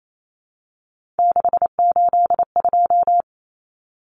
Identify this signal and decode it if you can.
Morse code, heard as a keyed tone, about 28 wpm, 700 Hz. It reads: 682
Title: Common Cut Numbers - 3-Digit 28wpm